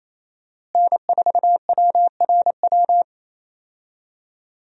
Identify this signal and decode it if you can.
Morse code, heard as a keyed tone, about 28 wpm, 700 Hz. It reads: N4WRW
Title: USA Call Signs - 5 Characters - Repeated 3x - 28wpm